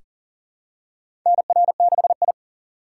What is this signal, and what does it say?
Morse code, heard as a keyed tone, about 39 wpm, 700 Hz. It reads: NR6I
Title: USA Call Signs - 4 Characters - 4x Character Spacing - 40wpm